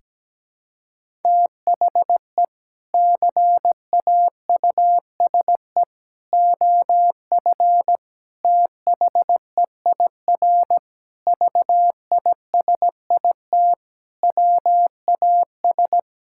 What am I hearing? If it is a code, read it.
Morse code, heard as a keyed tone, about 17 wpm, 700 Hz. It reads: THE CAUSE OF THEIR VISIT WAS
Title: Sentences from Top 1600 Words 17wpm